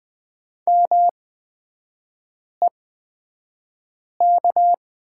Morse code, heard as a keyed tone, about 20 wpm, spaced as for 5 wpm, 700 Hz. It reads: MEK
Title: 224 Lesson 36e - BT - Three Characters 20wpm